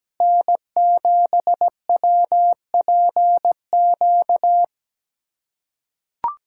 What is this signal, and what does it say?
Morse code, heard as a keyed tone, about 17 wpm, 700 Hz. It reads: N7WPQ
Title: USA Callsign DE Callsign 17wpm